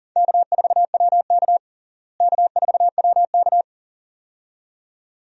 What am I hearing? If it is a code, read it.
Morse code, heard as a keyed tone, about 40 wpm, 700 Hz. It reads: K4WK K4WK
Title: USA Call Signs - 4 Characters - Repeated 2x - 40wpm